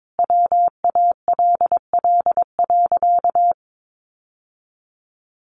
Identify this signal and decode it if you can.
Morse code, heard as a keyed tone, about 22 wpm, 700 Hz. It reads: WALL.
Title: Sentences from Top 1100 Words 22wpm